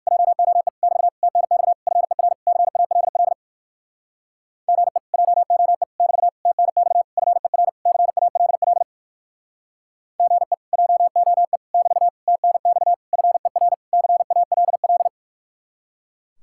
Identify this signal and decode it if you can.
Morse code, heard as a keyed tone, about 63 wpm, 700 Hz. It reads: JOE <BT> TNX FER CALL GE JOE <BT> TNX FER CALL GE JOE <BT> TNX FER CALL